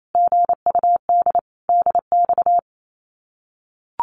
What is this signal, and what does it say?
Morse code, heard as a keyed tone, about 28 wpm, 700 Hz. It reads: GUD DX
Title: Common QSO Phrases Encore 28wpm